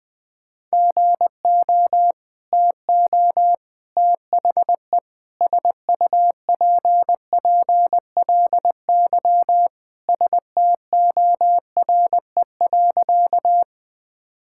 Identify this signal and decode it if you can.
Morse code, heard as a keyed tone, about 20 wpm, 700 Hz. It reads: GO TO THE SUPPLY STORE.
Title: Sentences from Top 1000 Words - Review Part 5 - 20wpm